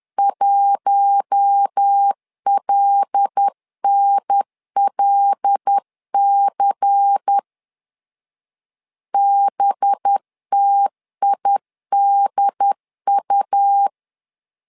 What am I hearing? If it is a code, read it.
Morse code, heard as a keyed tone, about 10 wpm, 800 Hz. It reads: 1LNLC BTIDU
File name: LEZIONI / 5 LBVC / 5 LBVC 10wpm 2349